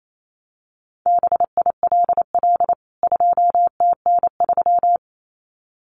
Morse code, heard as a keyed tone, about 28 wpm, 700 Hz. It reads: BILL 2TN3
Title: CWT Exchange 28wpm